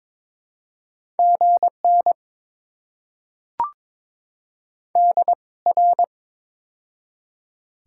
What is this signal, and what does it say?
Morse code, heard as a keyed tone, about 21 wpm, spaced as for 18 wpm, 700 Hz. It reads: GN DR
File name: Top 90 QSO Elements - 2x Character Spacing - 22wpm